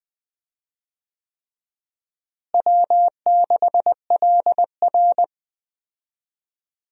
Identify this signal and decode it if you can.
Morse code, heard as a keyed tone, about 19 wpm, 700 Hz. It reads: W6LR
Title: USA Call Signs - 4 Characters - Repeated 3x - 20wpm